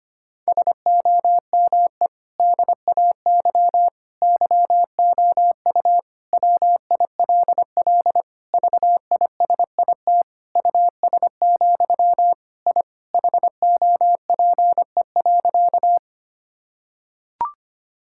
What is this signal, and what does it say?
Morse code, heard as a keyed tone, about 25 wpm, 700 Hz. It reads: SOME DAY YOU WILL VISIT US, I HOPE.
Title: Sentences from Top 1000 Words - Review Part 6 - 25wpm